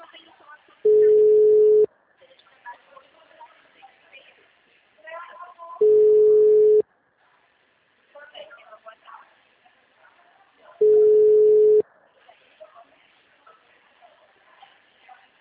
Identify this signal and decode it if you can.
Ringback tone (Europe, 425 Hz), with speech.
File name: out- -2009- - - .49720